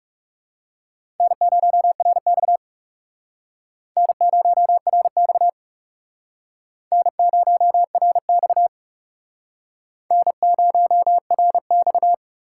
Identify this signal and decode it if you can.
Morse code, heard as a keyed tone, about 39 wpm, 700 Hz. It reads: N0RX N0RX N0RX N0RX